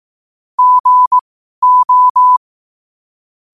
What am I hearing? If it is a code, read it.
Morse code, heard as a keyed tone, about 18 wpm, spaced as for 12 wpm, 1000 Hz. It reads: GO
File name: qotdfiles / 12wpm